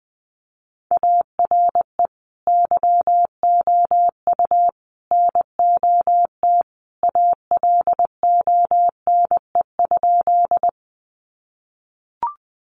Morse code, heard as a keyed tone, about 20 wpm, 700 Hz. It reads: ARE YOU NOT ALONE?
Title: Sentences from Top 1000 Words 20wpm